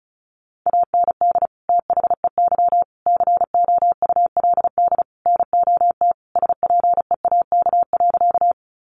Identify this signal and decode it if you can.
Morse code, heard as a keyed tone, about 34 wpm, 700 Hz. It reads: AND THEY COULD NOT SPEAK.
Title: Sentences from Top 1100 Words 35wpm